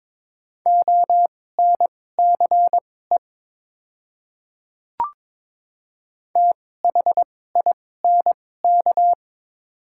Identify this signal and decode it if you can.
Morse code, heard as a keyed tone, about 21 wpm, spaced as for 16 wpm, 700 Hz. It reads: ONCE THINK
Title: 2x Character Spacing - Top 300 Words - 22wpm